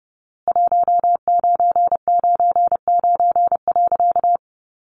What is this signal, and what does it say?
Morse code, heard as a keyed tone, about 30 wpm, 700 Hz. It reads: 1999.
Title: A Sound Of Thunder 30wpm